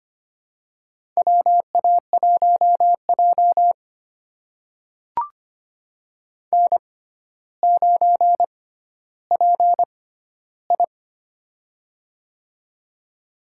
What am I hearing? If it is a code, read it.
Morse code, heard as a keyed tone, about 25 wpm, spaced as for 13 wpm, 700 Hz. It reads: WA1J N9PI
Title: USA Call Signs - 4 Characters - 6x Character Spacing - 25wpm